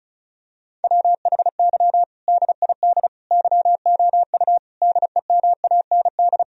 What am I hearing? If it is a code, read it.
Morse code, heard as a keyed tone, about 36 wpm, 700 Hz. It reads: WHY DID YOU DEMAND
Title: Sentences from Top 1000 Words - Review Part 4 - 35wpm